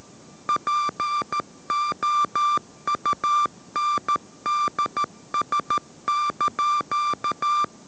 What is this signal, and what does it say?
Morse code, heard as a keyed tone, about 13 wpm, 1180 Hz. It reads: POUNDS)